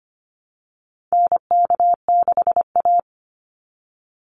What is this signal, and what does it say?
Morse code, heard as a keyed tone, about 25 wpm, 700 Hz. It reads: NK6A
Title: NA SOTA Chaser Callsigns 25wpm